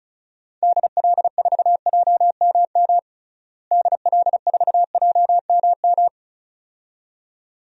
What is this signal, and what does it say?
Morse code, heard as a keyed tone, about 34 wpm, 700 Hz. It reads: DL4JMM DL4JMM
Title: DXCC-100 Call Signs - 6 Characters - Repeated 2x - 35wpm